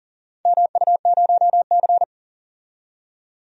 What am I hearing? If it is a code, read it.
Morse code, heard as a keyed tone, about 40 wpm, 700 Hz. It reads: MU0C
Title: DXCC-100 Call Signs - 4 Characters - 40wpm